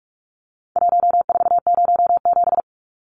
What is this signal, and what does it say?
Morse code, heard as a keyed tone, about 44 wpm, 700 Hz. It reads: 1407
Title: Numbers - Contest Style - 4-Digit 45wpm